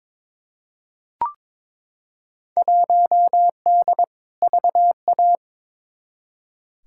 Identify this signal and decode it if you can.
Morse code, heard as a keyed tone, about 22 wpm, 700 Hz. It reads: 1D VA
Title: Field Day Exchanges - 22wpm